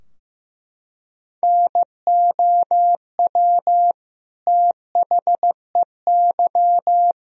Morse code, heard as a keyed tone, about 15 wpm, 700 Hz. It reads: NOW THEY
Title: Sentences from Top 1000 Words - Review Part 4 - 15wpm